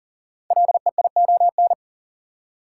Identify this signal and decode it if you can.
Morse code, heard as a keyed tone, about 40 wpm, 700 Hz. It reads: LEION
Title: Code Groups - 5 Letter:Numbers - 40wpm